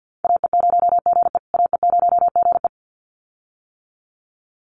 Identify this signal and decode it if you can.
Morse code, heard as a keyed tone, about 50 wpm, 700 Hz. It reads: AE0GE AE0GE
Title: USA Call Signs Repeated 2x 50wpm